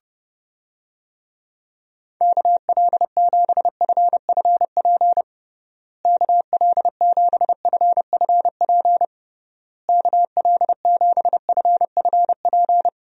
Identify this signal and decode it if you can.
Morse code, heard as a keyed tone, about 30 wpm, 700 Hz. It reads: KL7FFP KL7FFP KL7FFP
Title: USA Call Signs - 6 Characters - Repeated 3x - 30wpm